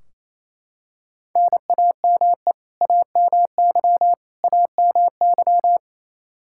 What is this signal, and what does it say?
Morse code, heard as a keyed tone, about 28 wpm, 700 Hz. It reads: NAME AMY AMY
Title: Practice QSO - Exchange 1 - Beginning - 28wpm